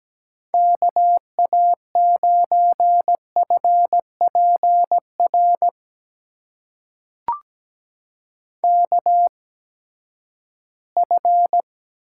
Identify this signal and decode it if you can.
Morse code, heard as a keyed tone, about 17 wpm, spaced as for 12 wpm, 700 Hz. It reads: KA9FPR KF
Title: USA Call Signs - 6 Characters - 8x Character Spacing - 17wpm